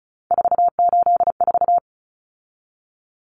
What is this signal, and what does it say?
Morse code, heard as a keyed tone, about 34 wpm, 700 Hz. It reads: 484
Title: Common Cut Numbers - 3-Digit 35wpm